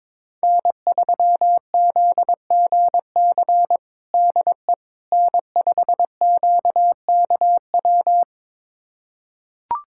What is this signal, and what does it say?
Morse code, heard as a keyed tone, about 22 wpm, 700 Hz. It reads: N3ZGC DE N5QKW
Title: USA Callsign DE Callsign - 5 Characters - 22wpm